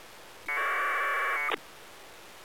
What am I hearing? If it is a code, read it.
Packet radio (AX.25, as in APRS).